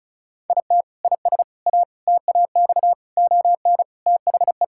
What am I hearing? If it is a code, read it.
Morse code, heard as a keyed tone, about 36 wpm, 700 Hz. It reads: IT IS A TAX ON THE
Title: Sentences from Top 1000 Words - Review Part 5 - 35wpm